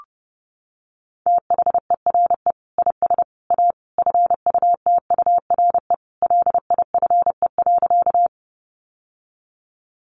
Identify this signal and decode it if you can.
Morse code, heard as a keyed tone, about 30 wpm, 700 Hz. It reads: THERE IS A FUTURE LIFE.
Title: Sentences from Top 700 Words 30wpm